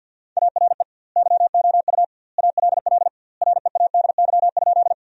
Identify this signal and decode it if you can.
Morse code, heard as a keyed tone, about 50 wpm, 700 Hz. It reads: ARE YOU ALL READY?
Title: 2x Word-Spacing - Sentences from Top 400 Words - 50wpm